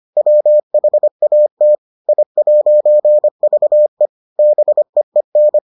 Morse code, heard as a keyed tone, about 25 wpm, 600 Hz. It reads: WHAT I'VE BEEN